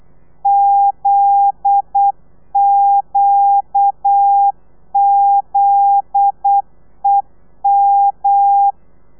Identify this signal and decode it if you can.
Morse code, heard as a keyed tone, about 8 wpm, 800 Hz. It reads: ZQZEM
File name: telegrafia / i2tfj / Lez9 35